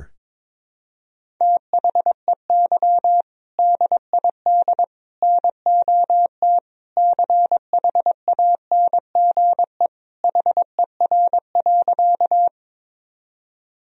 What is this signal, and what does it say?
Morse code, heard as a keyed tone, about 22 wpm, 700 Hz. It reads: THEY DID NOT CHANGE HER.
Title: Sentences from Top 1000 Words - Review Part 1 - 22wpm